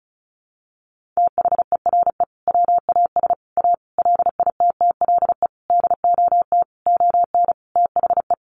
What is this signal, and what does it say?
Morse code, heard as a keyed tone, about 34 wpm, 700 Hz. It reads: THERE WAS A LITTLE DOT ON THE
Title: Sentences from Top 1500 Words 35wpm